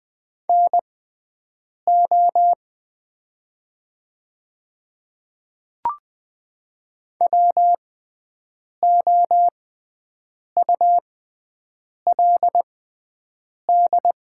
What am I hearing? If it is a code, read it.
Morse code, heard as a keyed tone, about 20 wpm, spaced as for 7 wpm, 700 Hz. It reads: NO WOULD
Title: 6x Character Spacing - Top 300 Words - 20wpm